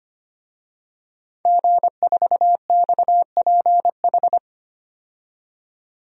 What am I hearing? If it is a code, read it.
Morse code, heard as a keyed tone, about 25 wpm, 700 Hz. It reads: G4XPH
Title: DXCC-100 Call Signs - 5 Characters - 25wpm